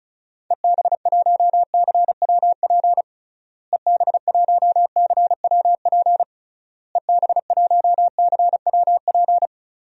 Morse code, heard as a keyed tone, about 34 wpm, 700 Hz. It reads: EB1CWP EB1CWP EB1CWP
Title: DXCC-100 Call Signs - 6 Characters - Repeated 3x - 35wpm